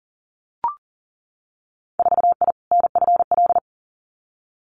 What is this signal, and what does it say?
Morse code, heard as a keyed tone, about 37 wpm, 700 Hz. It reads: 4I NFL
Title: Winter Field Day Exchanges 40wpm